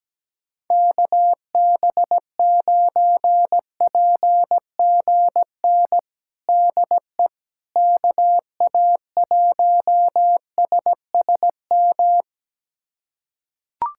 Morse code, heard as a keyed tone, about 17 wpm, 700 Hz. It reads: KB9PGN DE KA1SSM
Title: USA Callsign DE Callsign - 6 Characters - 17wpm